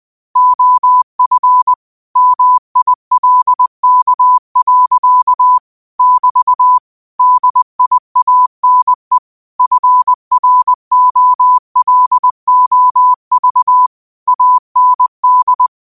Morse code, heard as a keyed tone, about 20 wpm, 1000 Hz. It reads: OF MILK. <BT> DIANE FROLOV AND